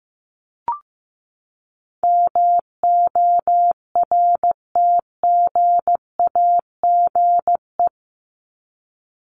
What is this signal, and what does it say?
Morse code, heard as a keyed tone, about 15 wpm, 700 Hz. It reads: MORTGAGE
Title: Top 1751-2000 Words 15wpm